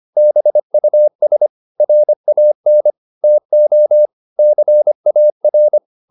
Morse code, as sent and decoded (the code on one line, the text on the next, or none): -... ..- ... / .-. .- -. / - --- / -.-. .- .-.
BUS RAN TO CAR